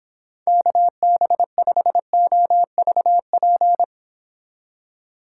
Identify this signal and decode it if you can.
Morse code, heard as a keyed tone, about 26 wpm, 700 Hz. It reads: KB5OVP